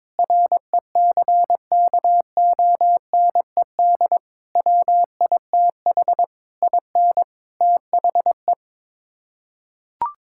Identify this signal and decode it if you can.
Morse code, heard as a keyed tone, about 22 wpm, 700 Hz. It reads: RECKONED WITH IN THE
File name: practice / 22wpm / Sets of 8 Words - 22wpm